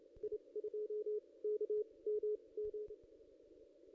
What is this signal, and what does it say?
Morse code, heard as a keyed tone, about 29 wpm, spaced as for 22 wpm, 440 Hz. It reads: I2KMM
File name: ss-cw-k5tr-op-n5rz